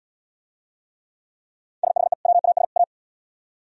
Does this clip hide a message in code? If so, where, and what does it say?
Yes: Morse code from 1.8 to 2.8 s, heard as a keyed tone, about 69 wpm, 700 Hz. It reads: SHE WAS A